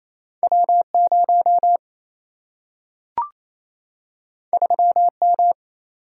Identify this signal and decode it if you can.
Morse code, heard as a keyed tone, about 28 wpm, 700 Hz. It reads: W0 3M
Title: Single-Letter Single-Number 28wpm